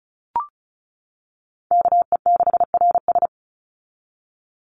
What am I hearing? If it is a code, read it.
Morse code, heard as a keyed tone, about 34 wpm, 700 Hz. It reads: KE6RS
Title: NA SOTA Chaser Callsigns 35wpm